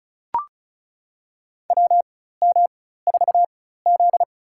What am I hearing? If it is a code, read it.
Morse code, heard as a keyed tone, about 36 wpm, spaced as for 16 wpm, 700 Hz. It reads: WM4Z
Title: USA Call Signs - 4 Characters - 4x Character Spacing - 35wpm